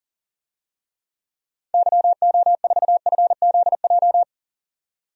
Morse code, heard as a keyed tone, about 40 wpm, 700 Hz. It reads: YO4FZJ
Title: DXCC-100 Call Signs - 6 Characters - 40wpm